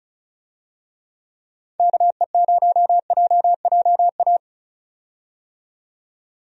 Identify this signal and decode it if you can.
Morse code, heard as a keyed tone, about 34 wpm, 700 Hz. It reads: KE0JJA
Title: USA Call Signs - 6 Characters - 35wpm